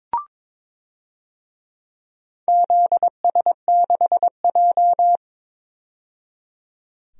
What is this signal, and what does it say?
Morse code, heard as a keyed tone, about 22 wpm, 700 Hz. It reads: ZS6J
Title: DXCC-100 Call Signs - 4 Characters - 22wpm